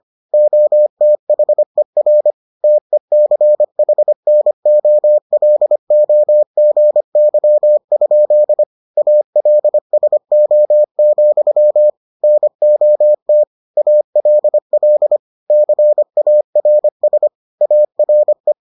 Morse code, heard as a keyed tone, about 25 wpm, 600 Hz. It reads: OTHER TECHNOLOGY? ALSO, NOT ALL CARS ARE